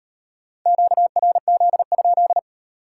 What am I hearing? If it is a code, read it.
Morse code, heard as a keyed tone, about 37 wpm, 700 Hz. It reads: QRZ?